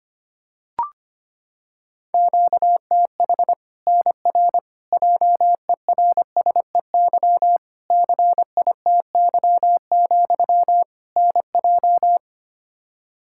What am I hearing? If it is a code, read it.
Morse code, heard as a keyed tone, about 25 wpm, 700 Hz. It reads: QTH NR JERSEY CITY, NJ
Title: K1USN SST QSO Practice 25wpm